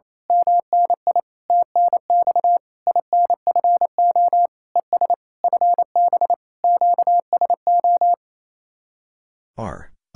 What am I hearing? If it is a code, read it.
Morse code, heard as a keyed tone, about 26 wpm, 700 Hz. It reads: MNI TNX INFO ES FB QSO
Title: Speed-Racing Common QSO Phrases Encore - 28wpm